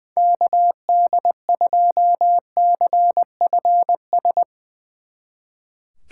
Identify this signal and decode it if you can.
Morse code, heard as a keyed tone, about 20 wpm, 700 Hz. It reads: KD2CFS
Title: Speed Racing - USA 2x3 Call Signs 20wpm